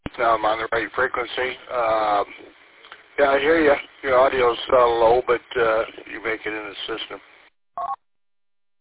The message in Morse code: none

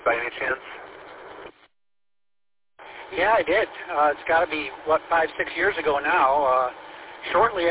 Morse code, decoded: none

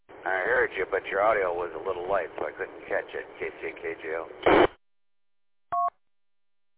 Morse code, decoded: none

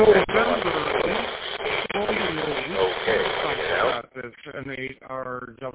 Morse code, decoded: none

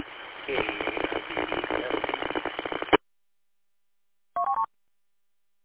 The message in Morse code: none

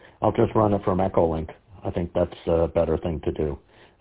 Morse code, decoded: none

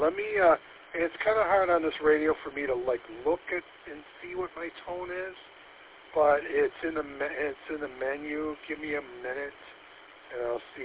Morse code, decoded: none

